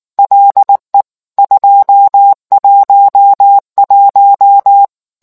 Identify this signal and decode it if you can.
Morse code, heard as a keyed tone, about 19 wpm, 800 Hz. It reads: LE 211